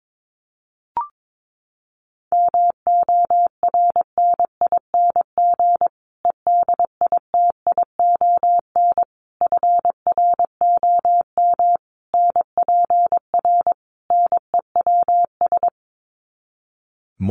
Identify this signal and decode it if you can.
Morse code, heard as a keyed tone, about 22 wpm, 700 Hz. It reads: MORNING EDITION FROM NPR NEWS
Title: Sets of 5 Words Encore 22wpm